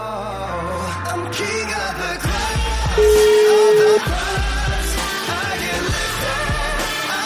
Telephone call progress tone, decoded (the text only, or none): ringback tone